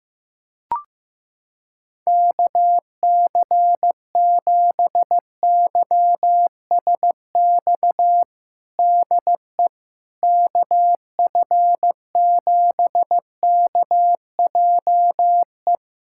Morse code, heard as a keyed tone, about 15 wpm, 700 Hz. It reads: KC7YSX DE KF7KJE
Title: USA Callsign DE Callsign - 6 Characters - 15wpm